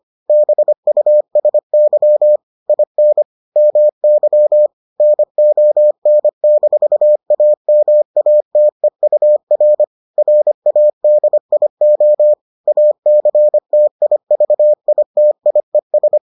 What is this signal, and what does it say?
Morse code, heard as a keyed tone, about 25 wpm, 600 Hz. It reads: BUSY IN MY NON-AMATEUR RADIO ACTIVITIES